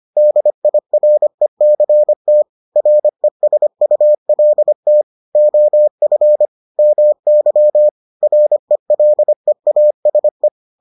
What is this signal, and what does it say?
Morse code, heard as a keyed tone, about 25 wpm, 600 Hz. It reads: DIRECT RESULT OF MY RELEASE